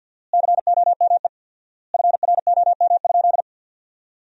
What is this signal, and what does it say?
Morse code, heard as a keyed tone, about 48 wpm, 700 Hz. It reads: COME FROM?
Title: 4x Word-Spacing - Sentences from Top 300 Words - 50wpm